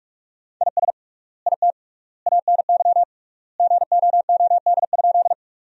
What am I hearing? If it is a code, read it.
Morse code, heard as a keyed tone, about 46 wpm, 700 Hz. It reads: IS IT ANY GOOD?
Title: 3x Word-Spacing - Sentences from Top 200 Words - 45wpm